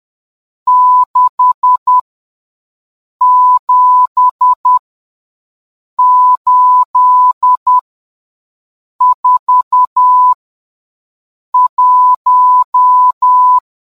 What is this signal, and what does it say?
Morse code, heard as a keyed tone, about 10 wpm, spaced as for 5 wpm, 1000 Hz. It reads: 67841